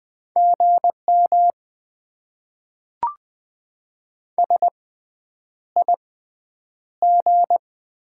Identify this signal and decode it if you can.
Morse code, heard as a keyed tone, about 20 wpm, spaced as for 9 wpm, 700 Hz. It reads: GM SIG
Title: Top 90 QSO Elements - 6x Character Spacing - 20wpm